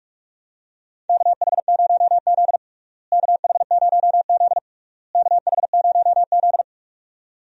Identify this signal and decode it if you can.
Morse code, heard as a keyed tone, about 45 wpm, 700 Hz. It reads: KH0Z KH0Z KH0Z
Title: USA Call Signs - 4 Characters - Repeated 3x - 45wpm